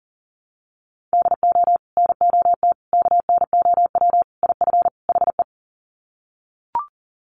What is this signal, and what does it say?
Morse code, heard as a keyed tone, about 40 wpm, 700 Hz. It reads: DO NOT KNOW IF HE
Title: Sets of 5 Words Encore2 40wpm